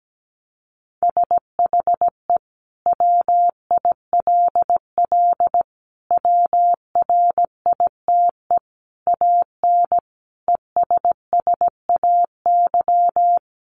Morse code, heard as a keyed tone, about 17 wpm, 700 Hz. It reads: SHE WILL WRITE AN ESSAY
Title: Sentences from Top 1500 Words 17wpm